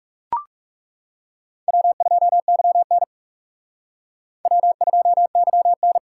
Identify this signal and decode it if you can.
Morse code, heard as a keyed tone, about 37 wpm, 700 Hz. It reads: W2YN W2YN
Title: Speed Racing - USA 1x2 Call Signs 30wpm